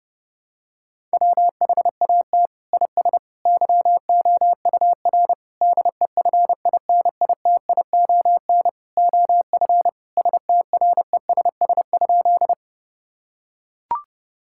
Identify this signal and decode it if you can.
Morse code, heard as a keyed tone, about 32 wpm, 700 Hz. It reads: WHAT IS YOUR DEFINITION OF STRESS?
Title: Sentences from Top 1000 Words - Review Part 6 - 30wpm